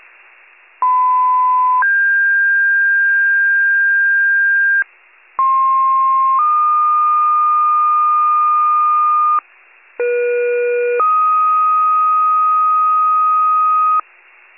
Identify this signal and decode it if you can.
Two-tone page, heard as a one-second tone, then a three-second tone, about 1010 then 1600 Hz, about 1060 then 1220 Hz, about 480 then 1250 Hz. It is heard from 0.8 to 14.0 s.
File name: pager / 98 RESCUE / wcalert